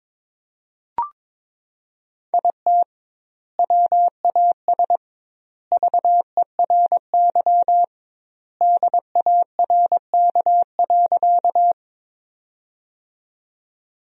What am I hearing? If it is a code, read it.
Morse code, heard as a keyed tone, about 22 wpm, 700 Hz. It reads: IT WAS VERY DARK.
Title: 2x Word-Spacing - Sentences from Top 500 Words - 22wpm